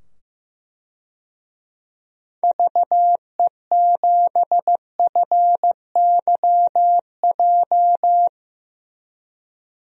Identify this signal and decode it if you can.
Morse code, heard as a keyed tone, about 15 wpm, 700 Hz. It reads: VE7FYJ
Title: DXCC-100 Call Signs - 6 Characters - 15wpm